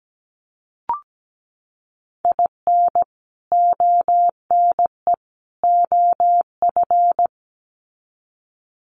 Morse code, heard as a keyed tone, about 17 wpm, 700 Hz. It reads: IN ONE OF
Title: Sets of 3 Words 17wpm